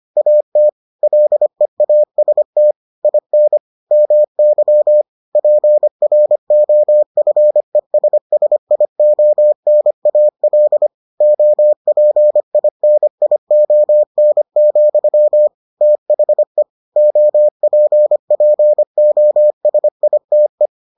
Morse code, heard as a keyed tone, about 25 wpm, 600 Hz. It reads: AT LEAST IN MY PROFESSIONAL OPINION, THE OPPOSITE